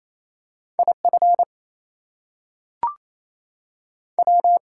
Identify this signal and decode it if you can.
Morse code, heard as a keyed tone, about 28 wpm, 700 Hz. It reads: IF W
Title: Two Letter Words - 28wpm